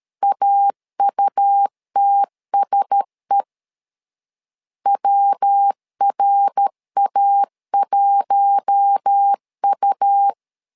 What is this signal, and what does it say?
Morse code, heard as a keyed tone, about 13 wpm, 800 Hz. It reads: AUTSE WRA1U
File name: LEZIONI / 4 Q1S 12wpm 1802